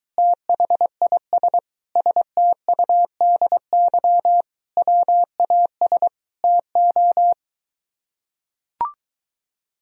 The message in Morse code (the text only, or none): THIS STUDY WAS TO